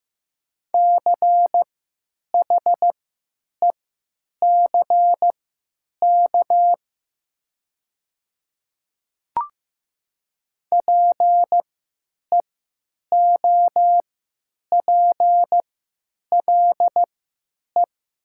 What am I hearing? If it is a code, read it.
Morse code, heard as a keyed tone, about 15 wpm, spaced as for 9 wpm, 700 Hz. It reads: CHECK PEOPLE
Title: 3x Character Spacing - Top 500 Words - 15wpm